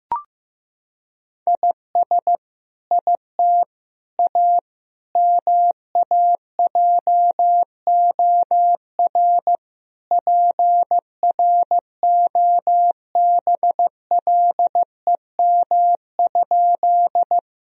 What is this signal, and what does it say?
Morse code, heard as a keyed tone, about 16 wpm, 700 Hz. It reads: IS IT A MAJOR PROBLEM?
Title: Sentences from Top 1000 Words - Review Part 4 - 15wpm